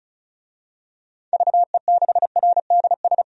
Morse code, heard as a keyed tone, about 36 wpm, 700 Hz. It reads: VE6RDS
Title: DXCC-100 Call Signs - 6 Characters - 35wpm